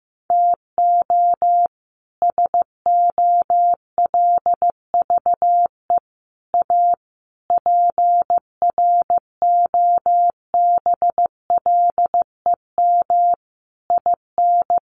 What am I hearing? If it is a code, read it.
Morse code, heard as a keyed tone, about 15 wpm, 700 Hz. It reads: TO SOLVE A PROBLEM IN
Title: Sentences from Top 1400 Words 15wpm